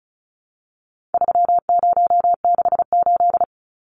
Morse code, heard as a keyed tone, about 34 wpm, 700 Hz. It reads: 3068
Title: Numbers - 4-Digit 35wpm